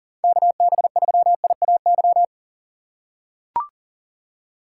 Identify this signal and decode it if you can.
Morse code, heard as a keyed tone, about 40 wpm, 700 Hz. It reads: KB3IAY
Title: USA Call Signs - 3x Character Spacing - 40wpm